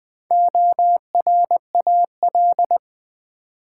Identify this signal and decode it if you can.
Morse code, heard as a keyed tone, about 20 wpm, 700 Hz. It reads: ORAL
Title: Build A Word - 2 Letters At A Time - 4-Letter Words - 20wpm